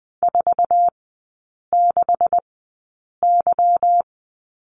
Morse code, heard as a keyed tone, about 20 wpm, spaced as for 8 wpm, 700 Hz. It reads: 46Y
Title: 166 Lesson 27e - Number 7 - Three Characters 20wpm